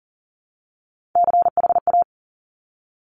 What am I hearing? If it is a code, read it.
Morse code, heard as a keyed tone, about 42 wpm, 690 Hz. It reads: CHA
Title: Top Three Letter Combinations in Words 40wpm